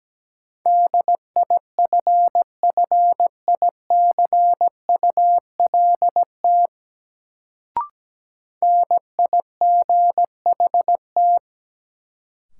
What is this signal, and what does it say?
Morse code, heard as a keyed tone, about 17 wpm, 700 Hz. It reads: DIFFICULT NIGHT
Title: Sets of 4 Words - New Word Drill - 17wpm